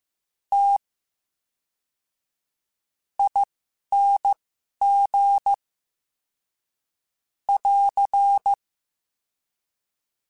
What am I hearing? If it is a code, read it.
Morse code, heard as a keyed tone, about 15 wpm, spaced as for 6 wpm, 800 Hz. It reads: TING <AR>
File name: Special INGs / Special INGs-mp3s / VerySlowToFaster Ing @10